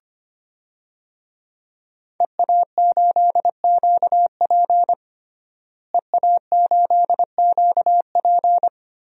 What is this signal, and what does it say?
Morse code, heard as a keyed tone, about 25 wpm, 700 Hz. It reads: EA8QP EA8QP
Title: DXCC-100 Call Signs - 5 Characters - Repeated 3x - 25wpm